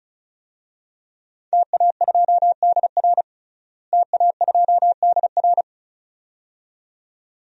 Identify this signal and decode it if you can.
Morse code, heard as a keyed tone, about 36 wpm, 700 Hz. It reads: TA2DR TA2DR
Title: DXCC-100 Call Signs - 5 Characters - Repeated 2x - 35wpm